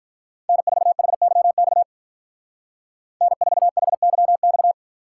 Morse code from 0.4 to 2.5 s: N4HYX

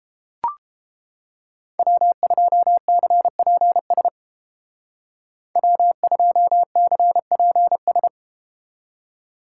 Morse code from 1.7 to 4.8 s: W2CPS

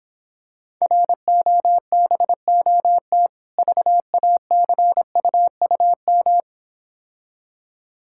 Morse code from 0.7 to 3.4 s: ROBOT

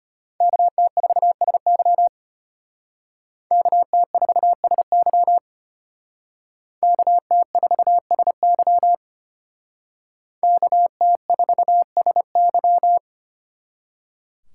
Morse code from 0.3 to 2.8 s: KT4SY

